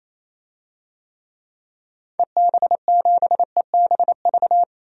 Morse code, heard as a keyed tone, about 28 wpm, 700 Hz. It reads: EB7EBV
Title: DXCC-100 Call Signs - 6 Characters - 28wpm